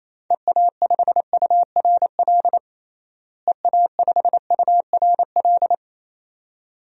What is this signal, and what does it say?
Morse code, heard as a keyed tone, about 28 wpm, 700 Hz. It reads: EA5URL EA5URL
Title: DXCC-100 Call Signs - 6 Characters - Repeated 2x - 28wpm